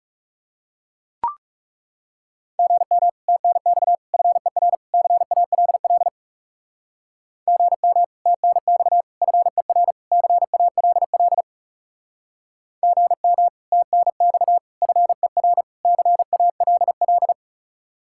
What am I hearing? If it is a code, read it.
Morse code, heard as a keyed tone, about 39 wpm, 700 Hz. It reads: GM TNX FER CALL GM TNX FER CALL GM TNX FER CALL